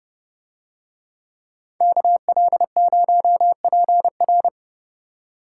Morse code, heard as a keyed tone, about 30 wpm, 700 Hz. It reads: KL0PR